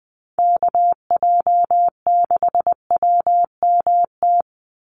Morse code, heard as a keyed tone, about 20 wpm, 700 Hz. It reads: KJ6WMT
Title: USA Call Signs 20wpm